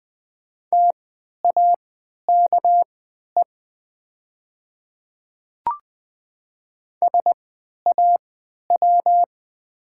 Morse code, heard as a keyed tone, about 20 wpm, spaced as for 11 wpm, 700 Hz. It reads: TAKE SAW
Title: 3x Character Spacing - Top 300 Words - 20wpm